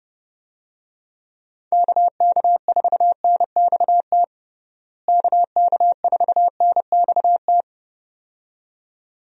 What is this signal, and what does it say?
Morse code, heard as a keyed tone, about 30 wpm, 700 Hz. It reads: KK4NXT KK4NXT
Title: USA Call Signs - 6 Characters - Repeated 2x - 30wpm